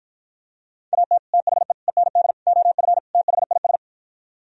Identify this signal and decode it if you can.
Morse code, heard as a keyed tone, about 53 wpm, 700 Hz. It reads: AT THE END OF THIS